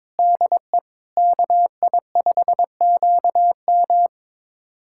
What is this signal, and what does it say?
Morse code, heard as a keyed tone, about 22 wpm, 700 Hz. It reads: DE KI5QM
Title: USA Callsign DE Callsign 22wpm